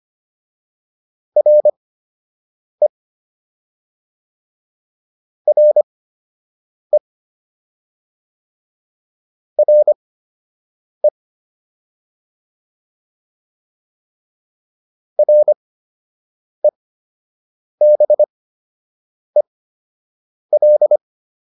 Morse code, heard as a keyed tone, about 25 wpm, spaced as for 7 wpm, 600 Hz. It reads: RE RE RE REBEL